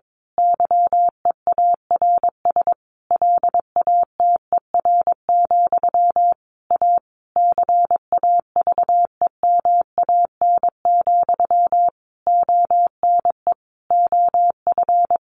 Morse code, heard as a keyed tone, about 22 wpm, 700 Hz. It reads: YEARS LATER, A CAVEMAN, ONE OF
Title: A Sound Of Thunder 22wpm